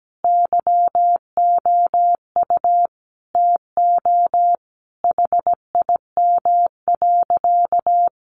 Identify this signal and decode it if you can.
Morse code, heard as a keyed tone, about 17 wpm, 700 Hz. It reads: YOU TO HIM.
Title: Sentences from Top 1200 Words 17wpm